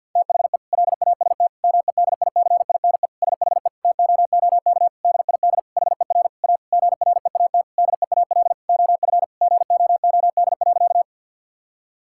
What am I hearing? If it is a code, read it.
Morse code, heard as a keyed tone, about 50 wpm, 700 Hz. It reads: THE LAST MEDICINE SHE TOOK DID HER A GREAT DEAL OF GOOD.
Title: Sentences from Top 1000 Words - Review Part 6 - 50wpm